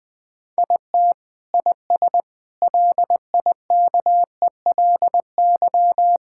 Morse code, heard as a keyed tone, about 20 wpm, 700 Hz. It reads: IT IS LIKELY
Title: Sentences with 2-Word Phrases - 20wpm